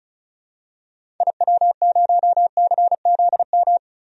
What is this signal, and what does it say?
Morse code, heard as a keyed tone, about 34 wpm, 700 Hz. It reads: IW0CZM
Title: DXCC-100 Call Signs - 6 Characters - 35wpm